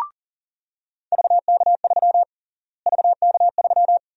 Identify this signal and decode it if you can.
Morse code, heard as a keyed tone, about 40 wpm, 700 Hz. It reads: VK3 VK3
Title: USA Call Sign Prefixes - Repeated 2x - 40wpm